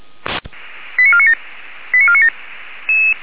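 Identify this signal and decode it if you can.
Five-tone selective call (ZVEI), heard as quick runs of five tones, tones of about 70 ms. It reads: 89497 89497
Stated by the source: ZVEI five-tone selective call received on a software-defined radio.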